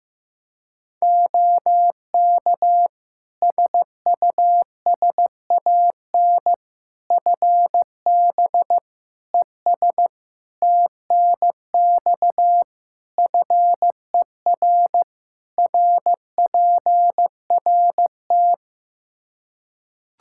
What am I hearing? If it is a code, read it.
Morse code, heard as a keyed tone, about 15 wpm, 700 Hz. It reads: OK SUSAN FB ES TNX FER RPRT
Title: Practice QSO - Exchange 2 - Info - 15wpm